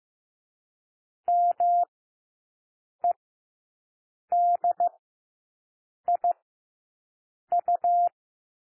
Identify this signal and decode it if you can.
Morse code, heard as a keyed tone, about 15 wpm, 700 Hz. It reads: M E D I U
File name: cwmp3 / W1 SOUND WORDS 15 wpm